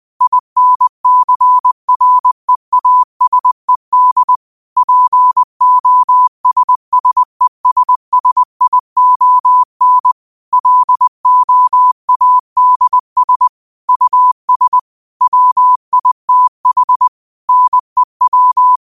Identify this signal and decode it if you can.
Morse code, heard as a keyed tone, about 20 wpm, 1000 Hz. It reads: INCREASED POSSESSION LOADS US WITH NEW
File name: qotdfiles / 20wpm